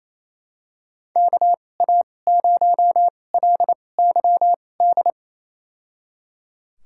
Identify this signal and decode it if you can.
Morse code, heard as a keyed tone, about 28 wpm, spaced as for 20 wpm, 700 Hz. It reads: KA0LYD
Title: USA Call Signs - 2x Character Spacing - 28wpm